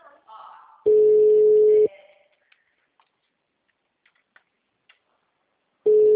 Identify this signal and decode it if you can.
Ringback tone (Europe, 425 Hz), with speech.
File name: out- -2019- - - .87151